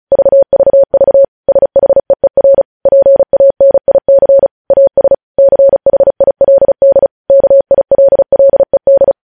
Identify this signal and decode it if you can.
Morse code, heard as a keyed tone, about 36 wpm, 550 Hz. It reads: VVV SHEER PANIC AS CHILD KILLED
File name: podcast / News headlines-1x-35x35